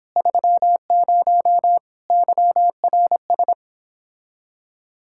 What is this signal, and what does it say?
Morse code, heard as a keyed tone, about 29 wpm, 700 Hz. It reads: 30 YRS